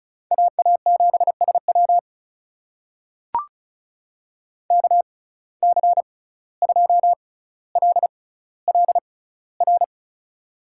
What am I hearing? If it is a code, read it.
Morse code, heard as a keyed tone, about 36 wpm, spaced as for 17 wpm, 700 Hz. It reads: AA7SW KC2LLR
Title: USA Call Signs - 6x Character Spacing - 35wpm